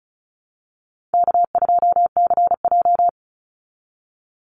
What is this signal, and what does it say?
Morse code, heard as a keyed tone, about 34 wpm, 700 Hz. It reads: K2CJ
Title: Call Signs Repeated 2x - 35wpm